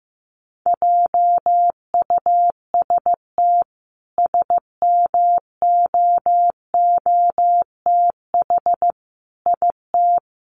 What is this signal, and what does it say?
Morse code, heard as a keyed tone, about 15 wpm, 700 Hz. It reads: JUST SMOOTH IT
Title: Sentences from Top 1400 Words 15wpm